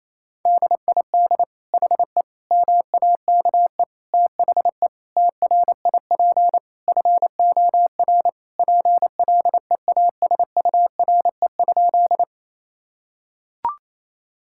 Morse code, heard as a keyed tone, about 28 wpm, 700 Hz. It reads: DID HE MAKE THE TRIP FOR PLEASURE?
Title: Sentences from Top 1600 Words 28wpm